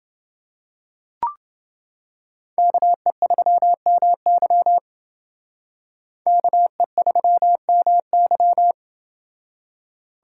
Morse code, heard as a keyed tone, about 29 wpm, 700 Hz. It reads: KE3MY KE3MY